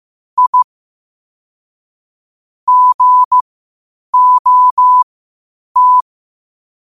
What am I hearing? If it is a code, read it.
Morse code, heard as a keyed tone, about 15 wpm, spaced as for 8 wpm, 1000 Hz. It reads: I GOT
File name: qotdfiles / 8wpm